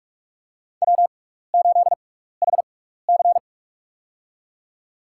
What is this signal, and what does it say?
Morse code, heard as a keyed tone, about 46 wpm, spaced as for 16 wpm, 700 Hz. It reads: W8HC
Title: USA Call Signs - 6x Character Spacing - 45wpm